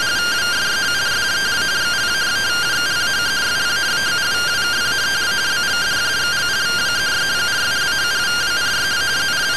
Two-tone teleprinter signal, about 170 Hz shift.